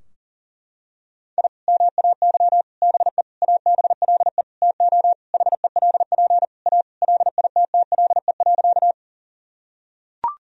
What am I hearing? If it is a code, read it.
Morse code, heard as a keyed tone, about 40 wpm, 700 Hz. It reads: I MAY BE ABLE TO HELP A LITTLE.
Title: Sentences from Top 1000 Words - Review Part 2 - 40wpm